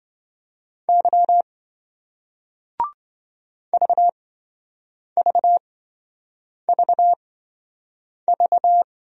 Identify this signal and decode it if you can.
Morse code, heard as a keyed tone, about 27 wpm, spaced as for 7 wpm, 700 Hz. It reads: Y VVVV